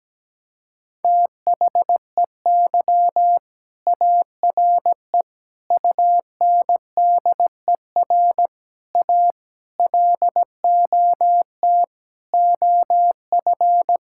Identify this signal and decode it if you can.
Morse code, heard as a keyed tone, about 17 wpm, 700 Hz. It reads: THEY ARE UNDER A LOT OF
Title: Sentences from Top 1000 Words - Review Part 6 - 17wpm